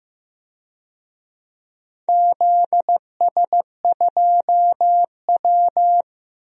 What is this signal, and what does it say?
Morse code, heard as a keyed tone, about 15 wpm, 700 Hz. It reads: ZS2W
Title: DXCC-100 Call Signs - 4 Characters - Repeated 3x - 15wpm